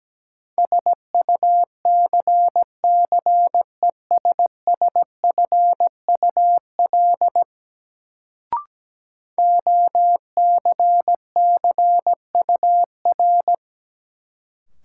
Morse code, heard as a keyed tone, about 17 wpm, 700 Hz. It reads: SUCCESSFUL OCCUR
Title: Sentences from Top 900 Words - New Word Drill - 17wpm